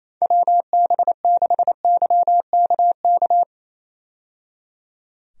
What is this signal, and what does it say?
Morse code, heard as a keyed tone, about 28 wpm, 700 Hz. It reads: WB6YKK
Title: USA Call Signs - 6 Characters - 28wpm